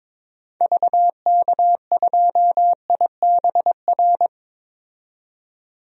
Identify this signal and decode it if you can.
Morse code, heard as a keyed tone, about 22 wpm, 700 Hz. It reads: VK2IBR
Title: DXCC-100 Call Signs - 6 Characters - 22wpm